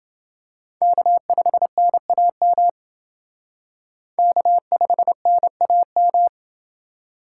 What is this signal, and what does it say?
Morse code, heard as a keyed tone, about 28 wpm, 700 Hz. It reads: K5NAM K5NAM